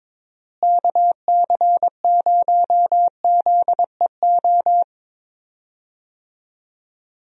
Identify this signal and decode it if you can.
Morse code, heard as a keyed tone, about 22 wpm, 700 Hz. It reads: KC0ZEO
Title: USA Call Signs - 6 Characters - Repeated 2x - 22wpm